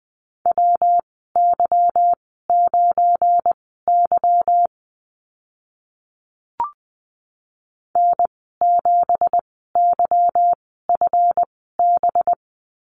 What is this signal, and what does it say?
Morse code, heard as a keyed tone, about 20 wpm, spaced as for 14 wpm, 700 Hz. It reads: WY9Y N7YFB
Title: 193 Lesson 31g - Letter v - Call Signs 20wpm